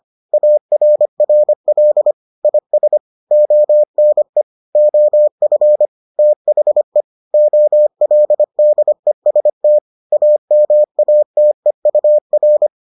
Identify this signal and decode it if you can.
Morse code, heard as a keyed tone, about 25 wpm, 600 Hz. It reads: ARRL IS ONE OF THE OLDEST AMATEUR